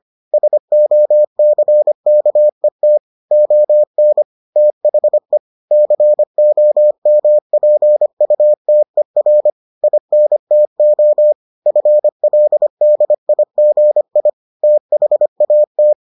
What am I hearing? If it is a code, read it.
Morse code, heard as a keyed tone, about 25 wpm, 600 Hz. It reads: SOCKET ON THE COMPUTER INTO FLDIGI THAT